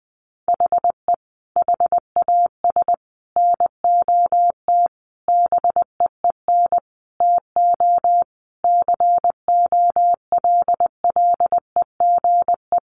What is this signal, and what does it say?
Morse code, heard as a keyed tone, about 20 wpm, 700 Hz. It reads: HE HAS NOT BEEN TO COLLEGE
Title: Sentences from Top 1300 Words 20wpm